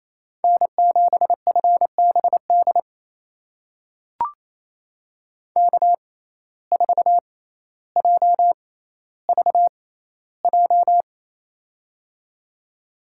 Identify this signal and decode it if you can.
Morse code, heard as a keyed tone, about 28 wpm, spaced as for 14 wpm, 700 Hz. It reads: N7FBD K4JVJ
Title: USA Call Signs - 6x Character Spacing - 28wpm